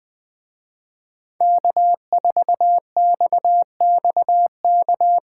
Morse code, heard as a keyed tone, about 21 wpm, 700 Hz. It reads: K4XXK
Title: Speed Racing - USA 1x3 Call Signs 17wpm